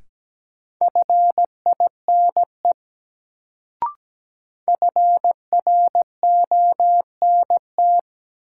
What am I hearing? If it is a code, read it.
Morse code, heard as a keyed tone, about 17 wpm, 700 Hz. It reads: FINE FRONT
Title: Sentences from Top 500 Words - New Word Drill - 17wpm